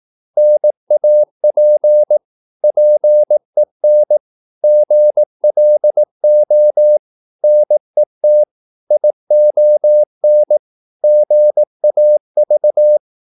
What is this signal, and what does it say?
Morse code, heard as a keyed tone, about 18 wpm, 600 Hz. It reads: NAP PEN GLO NET ION GAV
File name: Korte ord DK 18wpm